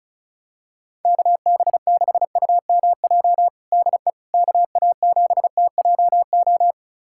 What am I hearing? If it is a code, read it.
Morse code, heard as a keyed tone, about 36 wpm, 700 Hz. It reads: KB6UMJ DE KA7TJO
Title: USA Callsign DE Callsign - 6 Characters - 35wpm